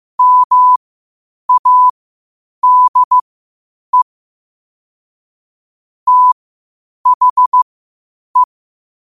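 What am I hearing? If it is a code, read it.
Morse code, heard as a keyed tone, about 15 wpm, spaced as for 8 wpm, 1000 Hz. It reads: MADE THE
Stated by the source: Morse practice at 8 WPM from Morse Resource.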